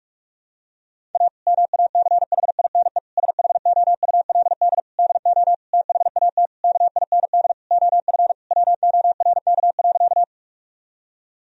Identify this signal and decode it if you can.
Morse code, heard as a keyed tone, about 46 wpm, 700 Hz. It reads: A MACHINE SHOULD DO THAT KIND OF WORK.
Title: Sentences from Top 1000 Words - Review Part 2 - 45wpm